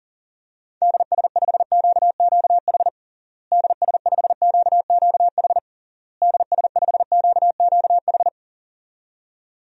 Morse code, heard as a keyed tone, about 40 wpm, 700 Hz. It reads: DS5QQH DS5QQH DS5QQH
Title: DXCC-100 Call Signs - 6 Characters - Repeated 3x - 40wpm